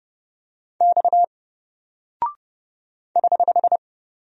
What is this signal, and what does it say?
Morse code, heard as a keyed tone, about 30 wpm, 700 Hz. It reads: X <HH>
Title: Single Character (Letter:Number:Punctuation:Prosign) Alt 30wpm